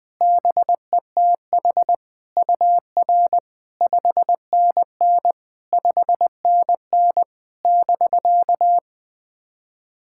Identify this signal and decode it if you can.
Morse code, heard as a keyed tone, about 20 wpm, 700 Hz. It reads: BETH UR 5NN 5NN <BK>
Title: SOTA QSO Practice 20wpm